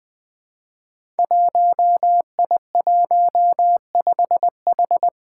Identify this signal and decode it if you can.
Morse code, heard as a keyed tone, about 20 wpm, 700 Hz. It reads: 1I15H
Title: Five Challenging Letter-Number Combinations 20wpm